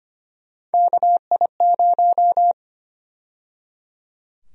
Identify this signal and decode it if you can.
Morse code, heard as a keyed tone, about 25 wpm, 700 Hz. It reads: KI0
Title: USA Call Sign Prefixes - 25wpm